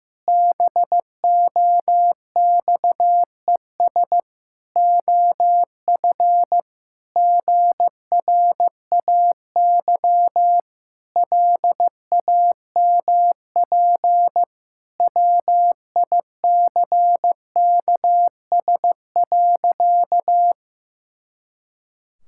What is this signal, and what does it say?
Morse code, heard as a keyed tone, about 14 wpm, 700 Hz. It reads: BOXES OF GRAY LAMP WICKS.
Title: Pangram Challenge v2 - 15wpm